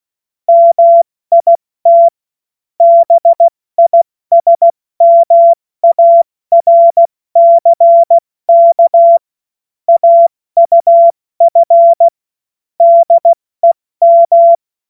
Morse code, heard as a keyed tone, about 16 wpm, 680 Hz. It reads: MIT BISMARCK AUF DEM
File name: Morsekurs / Witze / Witz 04. Januar 2022, 70 F80